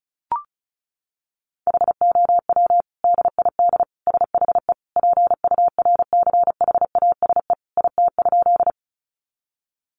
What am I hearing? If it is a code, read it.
Morse code, heard as a keyed tone, about 34 wpm, 700 Hz. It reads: HOW DID SHE PURCHASE IT?
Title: Sentences from Top 800 Words 35wpm